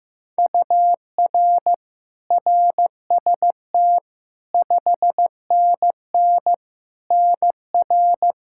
Morse code, heard as a keyed tone, about 15 wpm, 700 Hz. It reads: UR RST 5NN NR
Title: Practice QSO - Exchange 1 - Beginning - 15wpm